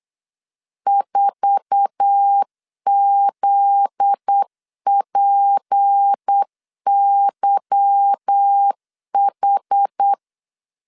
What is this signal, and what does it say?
Morse code, heard as a keyed tone, about 8 wpm, 800 Hz. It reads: 4ZPYH
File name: LEZIONI / 8 YJ45 / 8 YJ4 8wpm 3246